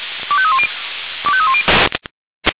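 Five-tone selective call (ZVEI), heard as quick runs of five tones, tones of about 70 ms. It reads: 25310 25310
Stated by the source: ZVEI five-tone selective call received on an SDR.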